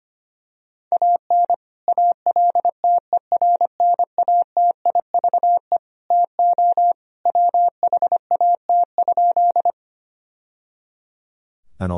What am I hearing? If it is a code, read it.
Morse code, heard as a keyed tone, about 25 wpm, 700 Hz. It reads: AN ALTERNATIVE TO WHAT?
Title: Sentences from Top 1000 Words - Review Part 4 - 25wpm